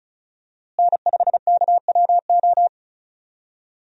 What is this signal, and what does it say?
Morse code, heard as a keyed tone, about 36 wpm, 700 Hz. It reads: N5KWO
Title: USA Call Signs - 5 Characters - Repeated 3x - 35wpm